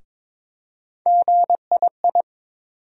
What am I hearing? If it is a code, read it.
Morse code, heard as a keyed tone, about 22 wpm, 700 Hz. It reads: GII
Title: USA Call Sign Postfixes - 3 Letters 22wpm